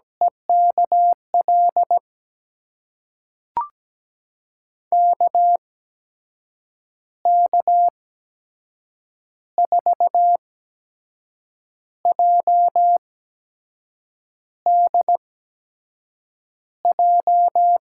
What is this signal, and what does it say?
Morse code, heard as a keyed tone, about 17 wpm, spaced as for 6 wpm, 700 Hz. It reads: EKL KK4JDJ
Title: USA Call Signs - 6 Characters - 8x Character Spacing - 17wpm